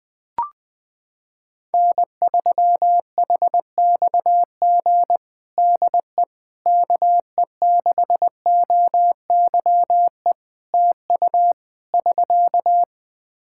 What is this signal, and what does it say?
Morse code, heard as a keyed tone, about 20 wpm, 700 Hz. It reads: N3HXG DE KE6OYE TU <SK>
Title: Practice QSO - Exchange 4 - Ending - 20wpm